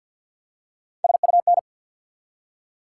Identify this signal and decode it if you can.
Morse code, heard as a keyed tone, about 53 wpm, 700 Hz. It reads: SUN
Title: Three Letter Words - 50wpm